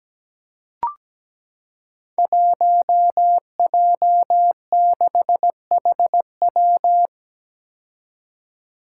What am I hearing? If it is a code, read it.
Morse code, heard as a keyed tone, about 17 wpm, 700 Hz. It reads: 1J6HW
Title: Five Challenging Letter-Number Combinations 17wpm